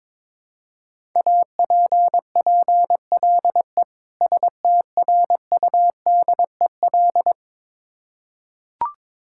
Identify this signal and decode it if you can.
Morse code, heard as a keyed tone, about 22 wpm, 700 Hz. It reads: APPLE STRUDEL
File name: Difficult Words with Hints - 22wpm